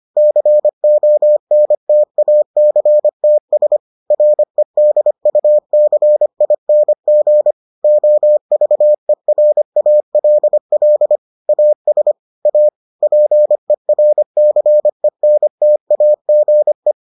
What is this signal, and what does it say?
Morse code, heard as a keyed tone, about 25 wpm, 600 Hz. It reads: CONTACTS REDUCING OVERALL AS A PERCENTAGE